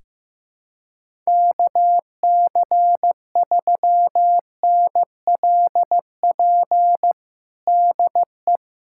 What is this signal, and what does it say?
Morse code, heard as a keyed tone, about 15 wpm, 700 Hz. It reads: KC3NLP DE
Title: USA Callsign DE Callsign 15wpm